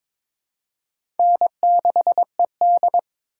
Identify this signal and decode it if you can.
Morse code, heard as a keyed tone, about 22 wpm, 700 Hz. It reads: N6ED
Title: USA Call Signs - 4 Characters - 3x Character Spacing - 22wpm